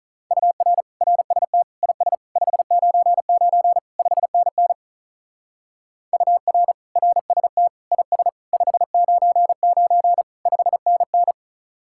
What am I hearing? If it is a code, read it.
Morse code, heard as a keyed tone, about 44 wpm, 700 Hz. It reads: UR RST IS 599 5NN UR RST IS 599 5NN